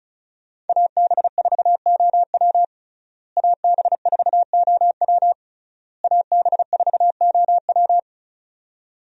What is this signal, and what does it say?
Morse code, heard as a keyed tone, about 36 wpm, 700 Hz. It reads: AB4OW AB4OW AB4OW
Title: USA Call Signs - 5 Characters - Repeated 3x - 35wpm